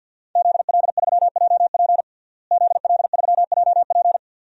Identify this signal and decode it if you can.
Morse code, heard as a keyed tone, about 47 wpm, 700 Hz. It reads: ZL3JP ZL3JP
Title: DXCC-100 Call Signs - 5 Characters - Repeated 2x - 50wpm